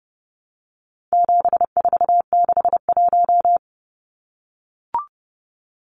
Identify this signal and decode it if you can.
Morse code, heard as a keyed tone, about 30 wpm, 700 Hz. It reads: 7461
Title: Numbers - 4-Digit 30wpm